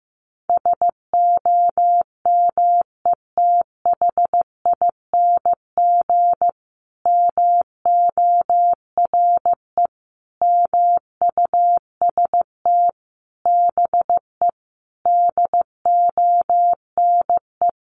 Morse code, heard as a keyed tone, about 14 wpm, 700 Hz. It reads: SOMETHING MORE MUST BE DONE
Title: Sentences from Top 1400 Words 15wpm